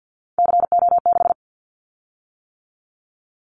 Morse code, heard as a keyed tone, about 50 wpm, 700 Hz. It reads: CO6
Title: Callsign Prefixes for Top 100 DXCC - 50wpm